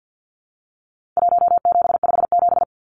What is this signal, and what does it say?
Morse code, heard as a keyed tone, about 49 wpm, 700 Hz. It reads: 1757
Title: Numbers - Contest Style - 4-Digit 50wpm